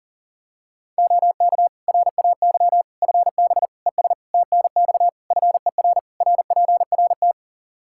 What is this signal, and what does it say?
Morse code, heard as a keyed tone, about 40 wpm, 700 Hz. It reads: OK RAY FB ES TNX FER RPRT
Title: Practice QSO - Exchange 2 - Info - 40wpm